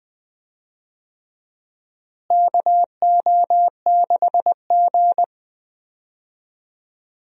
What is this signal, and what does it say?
Morse code, heard as a keyed tone, about 20 wpm, 700 Hz. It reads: KO6G
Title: USA Call Signs - 4 Characters - Repeated 3x - 20wpm